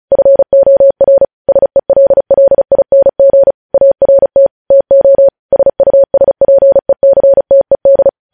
Morse code, heard as a keyed tone, about 36 wpm, 550 Hz. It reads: FOR SELLING ART TO SUSPECTED
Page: News Headlines in Morse Code at 35 WPM